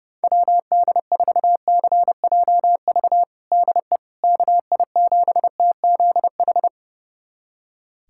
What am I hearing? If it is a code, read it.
Morse code, heard as a keyed tone, about 30 wpm, 700 Hz. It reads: WD4CJV DE KI7TZH
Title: USA Callsign DE Callsign - 6 Characters - 30wpm